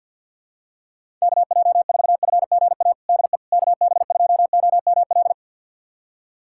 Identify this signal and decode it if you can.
Morse code, heard as a keyed tone, about 50 wpm, 700 Hz. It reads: KJ4FGA DE KB2OML
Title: USA Callsign DE Callsign - 6 Characters - 50wpm